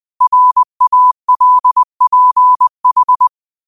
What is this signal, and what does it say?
Morse code, heard as a keyed tone, about 20 wpm, 1000 Hz. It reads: RALPH
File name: qotdfiles / 20wpm